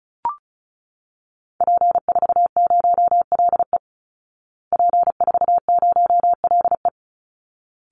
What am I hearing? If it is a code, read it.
Morse code, heard as a keyed tone, about 34 wpm, 700 Hz. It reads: P40LE P40LE
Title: Call Signs Repeated 3x - 35wpm